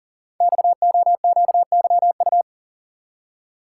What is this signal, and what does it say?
Morse code, heard as a keyed tone, about 43 wpm, 690 Hz. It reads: XOQYU
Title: Code Groups - 5 Letters - 40wpm